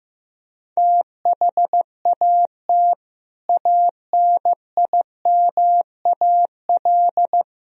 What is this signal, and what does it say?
Morse code, heard as a keyed tone, about 15 wpm, 700 Hz. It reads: THAT ANIMAL
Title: Sentences from Top 1000 Words - Review Part 1 - 15wpm